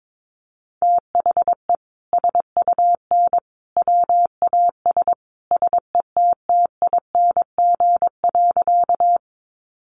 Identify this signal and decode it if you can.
Morse code, heard as a keyed tone, about 22 wpm, 700 Hz. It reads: THE SUN WAS SETTING.
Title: Sentences from Top 1300 Words 22wpm